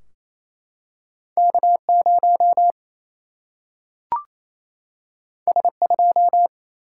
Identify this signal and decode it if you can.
Morse code, heard as a keyed tone, about 28 wpm, 700 Hz. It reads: K0 S2
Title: Single-Letter Single-Number 28wpm